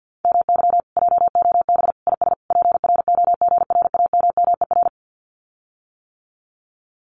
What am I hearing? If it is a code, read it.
Morse code, heard as a keyed tone, about 50 wpm, 700 Hz. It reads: MY JOB IS PROGRAMMER
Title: Common QSO Phrases Encore 50wpm